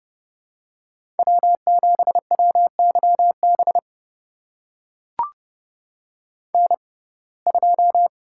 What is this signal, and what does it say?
Morse code, heard as a keyed tone, about 30 wpm, spaced as for 22 wpm, 700 Hz. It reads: W7WYB N2
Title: USA Call Signs - 5 Characters - 6x Character Spacing - 30wpm